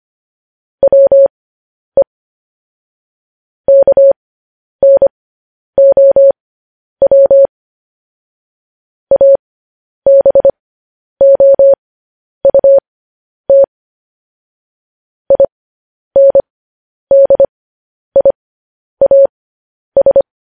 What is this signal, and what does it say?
Morse code, heard as a keyed tone, about 25 wpm, spaced as for 10 wpm, 550 Hz. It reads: WE KNOW ABOUT INDIAS
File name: podcast / News headlines-1x-25x10